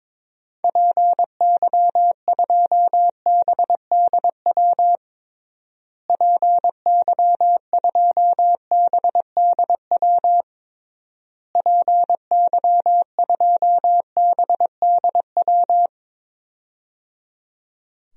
Morse code, heard as a keyed tone, about 22 wpm, 700 Hz. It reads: PY2BDW PY2BDW PY2BDW
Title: DXCC-100 Call Signs - 6 Characters - Repeated 3x - 22wpm